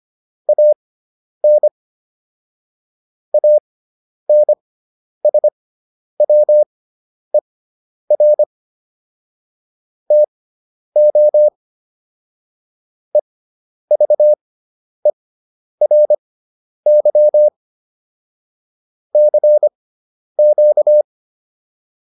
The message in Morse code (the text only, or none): AN ANSWER TO EVERY CQ